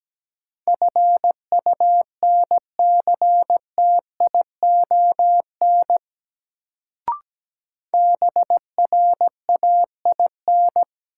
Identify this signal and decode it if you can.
Morse code, heard as a keyed tone, about 17 wpm, 700 Hz. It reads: FUNCTION BRAIN
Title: Sentences from Top 1100 Words - New Word Drill Alt - 17wpm